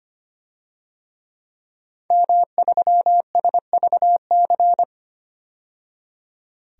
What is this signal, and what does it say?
Morse code, heard as a keyed tone, about 25 wpm, 700 Hz. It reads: M3SVC
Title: DXCC-100 Call Signs - 5 Characters - 25wpm